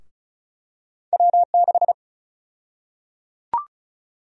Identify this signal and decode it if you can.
Morse code, heard as a keyed tone, about 34 wpm, 700 Hz. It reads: W6
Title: USA Call Sign Prefixes - Repeated 2x - 35wpm